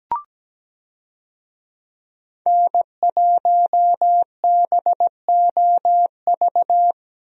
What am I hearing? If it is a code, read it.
Morse code, heard as a keyed tone, about 17 wpm, 700 Hz. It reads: N1BOV
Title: USA Call Signs - 5 Characters - Repeated 2x - 17wpm